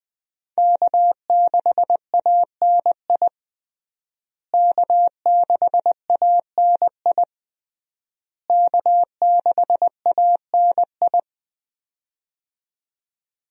Morse code, heard as a keyed tone, about 20 wpm, 700 Hz. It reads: K6ANI K6ANI K6ANI
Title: USA Call Signs - 5 Characters - Repeated 3x - 20wpm